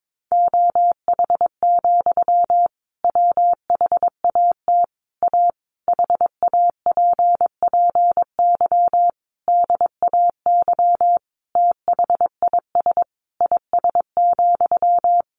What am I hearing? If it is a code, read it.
Morse code, heard as a keyed tone, about 22 wpm, 700 Hz. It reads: OH, WHAT A HAPPY DAY THIS IS,
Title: Sentences from Top 1500 Words 22wpm